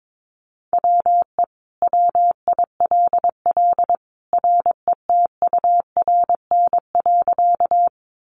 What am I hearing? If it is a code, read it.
Morse code, heard as a keyed tone, about 22 wpm, 700 Hz. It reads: WE WILL RETURN.
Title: Sentences from Top 800 Words 22wpm